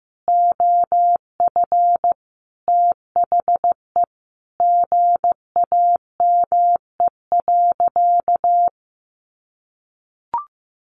Morse code, heard as a keyed tone, about 15 wpm, 700 Hz. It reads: OF THE GAME.
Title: Sentences from Top 1400 Words 15wpm